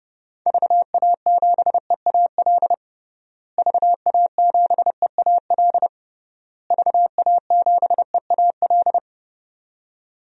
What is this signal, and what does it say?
Morse code, heard as a keyed tone, about 30 wpm, 700 Hz. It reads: VA7EAL VA7EAL VA7EAL
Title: DXCC-100 Call Signs - 6 Characters - Repeated 3x - 30wpm